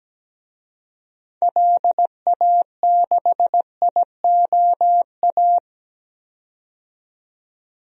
Morse code, heard as a keyed tone, about 17 wpm, 700 Hz. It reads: LA6IOA
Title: DXCC-100 Call Signs - 6 Characters - 17wpm